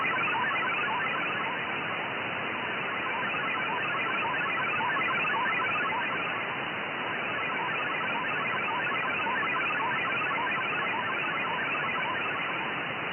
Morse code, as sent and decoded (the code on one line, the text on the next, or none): none